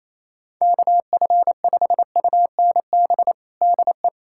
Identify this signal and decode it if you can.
Morse code, heard as a keyed tone, about 28 wpm, 700 Hz. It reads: KF5UNB DE
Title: USA Callsign DE Callsign - 6 Characters - 28wpm